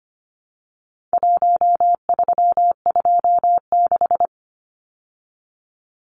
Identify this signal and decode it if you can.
Morse code, heard as a keyed tone, about 25 wpm, 700 Hz. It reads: 1326
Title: Numbers - Contest Style - 4-Digit 25wpm